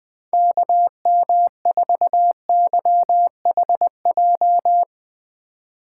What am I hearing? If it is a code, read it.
Morse code, heard as a keyed tone, about 20 wpm, 700 Hz. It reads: KM4YHJ
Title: USA Call Signs - 3x Character Spacing - 20wpm